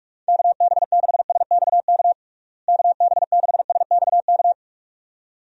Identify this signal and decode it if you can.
Morse code, heard as a keyed tone, about 46 wpm, 700 Hz. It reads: KB6SXK KB6SXK
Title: USA Call Signs - 6 Characters - Repeated 2x - 45wpm